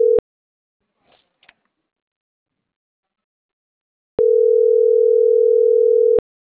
Ringback tone (North America), 440 + 480 Hz.